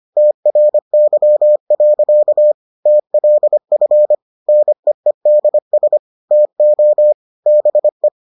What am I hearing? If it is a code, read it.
Morse code, heard as a keyed tone, about 25 wpm, 600 Hz. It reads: TRY. TLF NEEDS TO BE